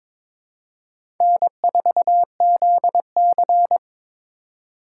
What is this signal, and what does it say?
Morse code, heard as a keyed tone, about 22 wpm, 700 Hz. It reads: N4ZC
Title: USA Call Signs - 4 Characters - 3x Character Spacing - 22wpm